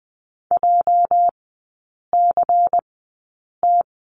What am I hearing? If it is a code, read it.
Morse code, heard as a keyed tone, about 20 wpm, spaced as for 8 wpm, 690 Hz. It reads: JCT
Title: 205 Lesson 33e - Letter j - Three Characters 20wpm